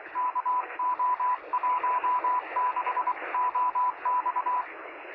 Morse code, heard as a keyed tone, about 23 wpm, 1000 Hz, with radio noise. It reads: KO16OX
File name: sm7lcb / 0630z yl2aj 23cm